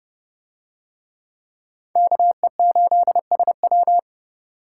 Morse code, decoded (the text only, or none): KE8SW